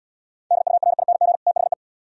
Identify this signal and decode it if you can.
Morse code, heard as a keyed tone, about 77 wpm, 700 Hz. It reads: DURING THE